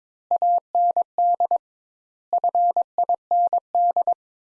Morse code, heard as a keyed tone, about 22 wpm, 700 Hz. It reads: AND FIND
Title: Sentences Top 200 Words - 22wpm